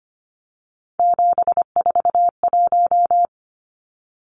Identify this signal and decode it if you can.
Morse code, heard as a keyed tone, about 25 wpm, 700 Hz. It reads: 741
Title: Numbers - 3-Digit 25wpm